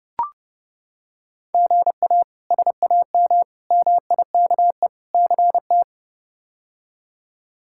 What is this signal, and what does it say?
Morse code, heard as a keyed tone, about 30 wpm, 700 Hz. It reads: GA SAM MIKE CT
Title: K1USN SST QSO Practice - 30wpm